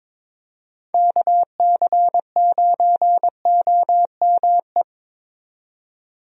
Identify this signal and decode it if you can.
Morse code, heard as a keyed tone, about 21 wpm, 700 Hz. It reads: KC9OME
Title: USA Call Signs - 6 Characters - 22wpm